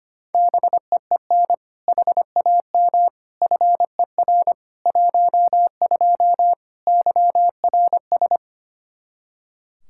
Morse code, heard as a keyed tone, about 25 wpm, 700 Hz. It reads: BEEN HAM FER 12 YRS
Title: Practice QSO - Exchange 3 - More Info - 25wpm